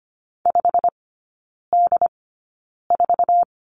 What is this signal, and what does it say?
Morse code, heard as a keyed tone, about 25 wpm, spaced as for 9 wpm, 700 Hz. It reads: 5D4
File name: Lesson 15e - Number 5 - Three Characters 25wpm